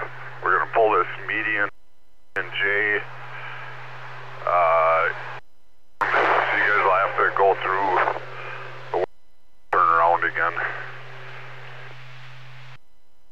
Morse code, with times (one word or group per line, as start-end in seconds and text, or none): none